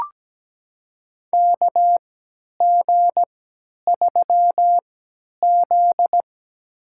Morse code, heard as a keyed tone, about 17 wpm, spaced as for 10 wpm, 700 Hz. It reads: KG3Z
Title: USA Call Signs - 4 Characters - 3x Character Spacing - 17wpm